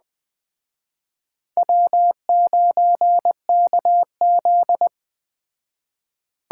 Morse code, heard as a keyed tone, about 20 wpm, 700 Hz. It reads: W9KZ
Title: Speed Racing - USA Call Signs 17wpm